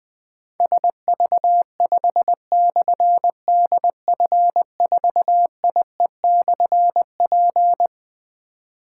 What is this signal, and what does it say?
Morse code, heard as a keyed tone, about 20 wpm, 700 Hz. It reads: SV5/DF4IE/P
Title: Difficult Call Signs 20wpm